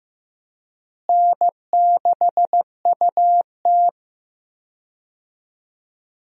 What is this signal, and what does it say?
Morse code, heard as a keyed tone, about 15 wpm, 700 Hz. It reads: N6UT
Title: USA Call Signs - 4 Characters - Repeated 2x - 15wpm